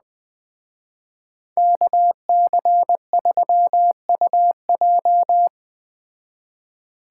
Morse code, heard as a keyed tone, about 20 wpm, 700 Hz. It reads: KC3UJ
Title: Speed Racing - USA 2x2 Call Signs 17wpm